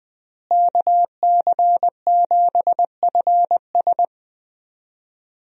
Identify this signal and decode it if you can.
Morse code, heard as a keyed tone, about 21 wpm, 700 Hz. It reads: KC7FS
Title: USA Call Signs - 5 Characters - Repeated 3x - 20wpm